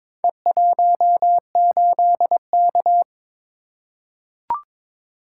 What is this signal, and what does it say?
Morse code, heard as a keyed tone, about 22 wpm, 700 Hz. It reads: E18K
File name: practice / 22wpm / Four Letter-Number Combinations - 22wpm